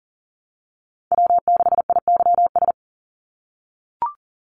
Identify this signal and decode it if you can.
Morse code, heard as a keyed tone, about 40 wpm, 700 Hz. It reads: W6IYS
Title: NA SOTA Chaser Callsigns 40wpm